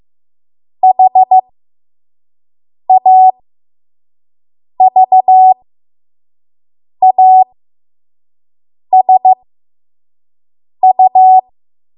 Morse code, heard as a keyed tone, about 15 wpm, spaced as for 5 wpm, 750 Hz. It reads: HAVASU
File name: Archive / 5 WPM / 05WPM